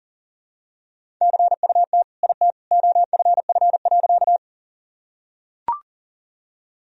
Morse code, heard as a keyed tone, about 40 wpm, 700 Hz. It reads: CUT IT OFF.
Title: Sentences from Top 1000 Words - Review Part 1 - 40wpm